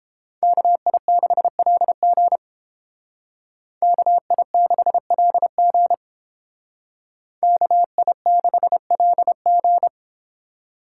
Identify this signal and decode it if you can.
Morse code, heard as a keyed tone, about 29 wpm, 700 Hz. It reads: KI6LG KI6LG KI6LG